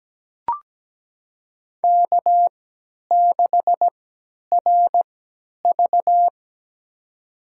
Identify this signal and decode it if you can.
Morse code, heard as a keyed tone, about 17 wpm, spaced as for 10 wpm, 700 Hz. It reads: K6RV
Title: USA Call Signs - 4 Characters - 3x Character Spacing - 17wpm